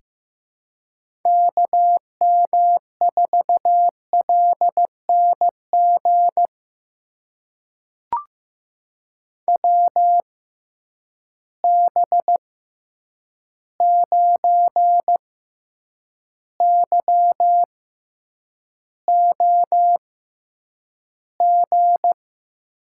Morse code, heard as a keyed tone, about 15 wpm, spaced as for 8 wpm, 700 Hz. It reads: KM4LNG WB9YOG
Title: USA Call Signs - 6 Characters - 6x Character Spacing - 15wpm